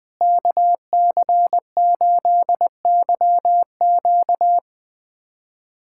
Morse code, heard as a keyed tone, about 20 wpm, 700 Hz. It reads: KC8YQ
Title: USA Callsign DE Callsign - 5 Characters - 20wpm